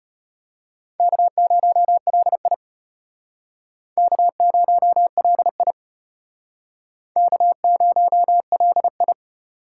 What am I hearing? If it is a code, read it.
Morse code, heard as a keyed tone, about 32 wpm, 700 Hz. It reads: K0LI K0LI K0LI